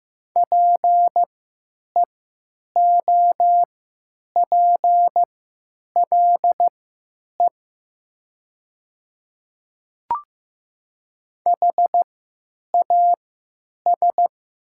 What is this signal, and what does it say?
Morse code, heard as a keyed tone, about 15 wpm, spaced as for 8 wpm, 700 Hz. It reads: PEOPLE HAS
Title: 3x Character Spacing - Top 100 Words - 15wpm